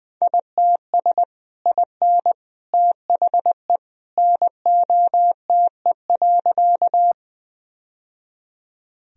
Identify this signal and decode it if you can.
Morse code, heard as a keyed tone, about 21 wpm, 700 Hz. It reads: ITS IN THE NOTE.
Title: Sentences from Top 1000 Words - Review Part 2 - 20wpm